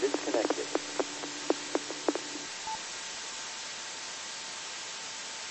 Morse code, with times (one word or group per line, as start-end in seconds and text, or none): none